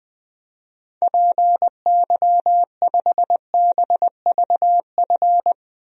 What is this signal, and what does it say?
Morse code, heard as a keyed tone, about 20 wpm, 700 Hz. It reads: PY5BVF
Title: DXCC-100 Call Signs - 6 Characters - Repeated 3x - 20wpm